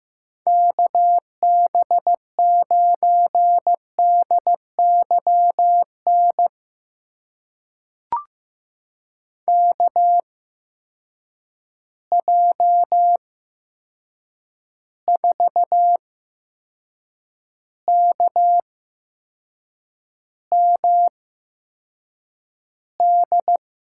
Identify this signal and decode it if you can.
Morse code, heard as a keyed tone, about 15 wpm, spaced as for 6 wpm, 700 Hz. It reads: KB9DYN KJ4KMD
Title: USA Call Signs - 6 Characters - 8x Character Spacing - 15wpm